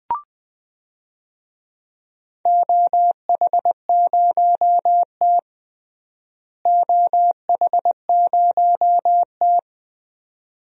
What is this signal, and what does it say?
Morse code, heard as a keyed tone, about 20 wpm, 700 Hz. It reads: OH0T OH0T
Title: DXCC-100 Call Signs - 4 Characters - Repeated 2x - 20wpm